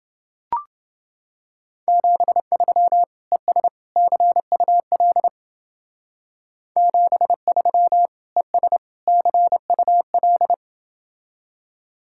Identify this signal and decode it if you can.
Morse code, heard as a keyed tone, about 28 wpm, 700 Hz. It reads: 73 ES CUL 73 ES CUL